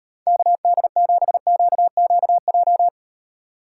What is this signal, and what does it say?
Morse code, heard as a keyed tone, about 38 wpm, 700 Hz. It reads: KD7QQJ